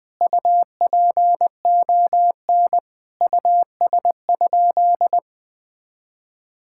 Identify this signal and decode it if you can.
Morse code, heard as a keyed tone, about 21 wpm, 700 Hz. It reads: UPON US?
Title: Sentences from Top 1600 Words 20wpm